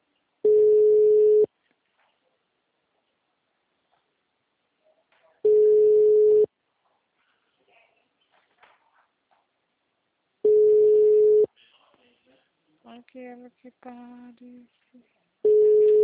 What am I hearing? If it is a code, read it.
Ringback tone (Europe, 425 Hz), with speech.